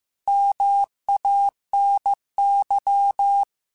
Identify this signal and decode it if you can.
Morse code, heard as a keyed tone, about 15 wpm, 800 Hz. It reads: MANY
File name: Technical Subjects / Technical Subjects-mp3s / AND Gate @13